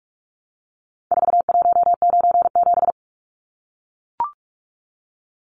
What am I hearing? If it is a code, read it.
Morse code, heard as a keyed tone, about 44 wpm, 700 Hz. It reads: 4197
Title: Numbers - 4-Digit 45wpm